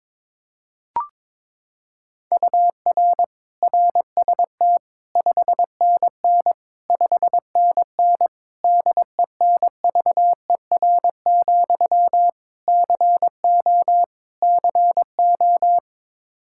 Morse code, heard as a keyed tone, about 22 wpm, 700 Hz. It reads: UR RST 5NN 5NN DENVER, CO CO
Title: Practice QSO - Exchange 1 - Beginning - 22wpm